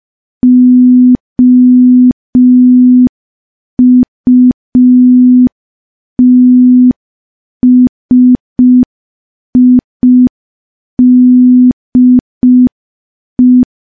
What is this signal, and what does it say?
Morse code, heard as a keyed tone, about 5 wpm, 250 Hz. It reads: OUTSIDE
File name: morse / morse-training-05WPM